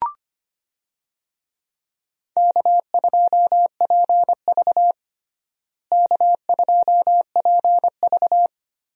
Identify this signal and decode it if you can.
Morse code, heard as a keyed tone, about 25 wpm, 700 Hz. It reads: K2PV K2PV
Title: USA Call Signs - 4 Characters - Repeated 3x - 25wpm